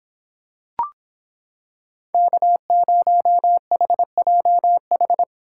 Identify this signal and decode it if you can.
Morse code, heard as a keyed tone, about 26 wpm, 700 Hz. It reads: K0HJH